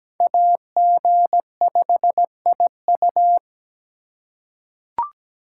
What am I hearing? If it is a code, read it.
Morse code, heard as a keyed tone, about 17 wpm, 700 Hz. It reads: AG5IU
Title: USA Call Signs - 5 Characters - 2x Character Spacing - 17wpm